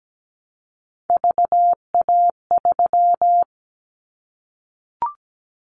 Morse code, heard as a keyed tone, about 17 wpm, 700 Hz. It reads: VA3
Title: Callsign Prefixes for Top 50 DXCC - 17wpm